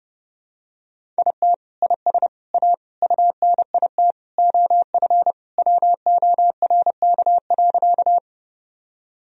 Morse code, heard as a keyed tone, about 30 wpm, 700 Hz. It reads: IT IS A UNIT OF WORK.
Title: Sentences from Top 1000 Words - Review Part 2 - 30wpm